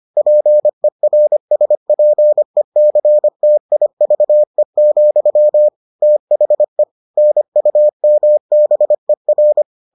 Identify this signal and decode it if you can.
Morse code, heard as a keyed tone, about 25 wpm, 600 Hz. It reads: PERSPECTIVE, THE NUMBER